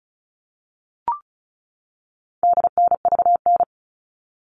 Morse code, heard as a keyed tone, about 34 wpm, 700 Hz. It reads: DNVN
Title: All Cut Numbers - 4-Digit 35wpm